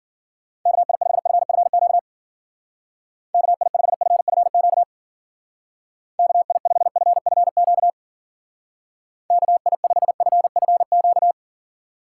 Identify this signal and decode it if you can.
Morse code, heard as a keyed tone, about 50 wpm, 700 Hz. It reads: KI5FFQ KI5FFQ KI5FFQ KI5FFQ
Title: Speed Racing - USA 2x3 Call Signs 40wpm